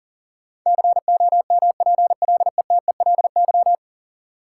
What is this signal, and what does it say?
Morse code, heard as a keyed tone, about 40 wpm, 700 Hz. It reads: COMPLETELY
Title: Sentences from Top 1100 Words - New Word Drill - 40wpm